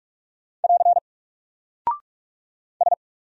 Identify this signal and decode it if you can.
Morse code, heard as a keyed tone, about 44 wpm, 700 Hz. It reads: <AR> S
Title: Single Character (Letter:Number:Punctuation:Prosign) 45wpm